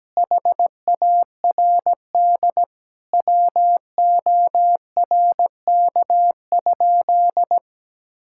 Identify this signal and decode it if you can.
Morse code, heard as a keyed tone, about 17 wpm, 700 Hz. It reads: HARD WORK?
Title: Sentences from Top 1000 Words - Review Part 6 - 17wpm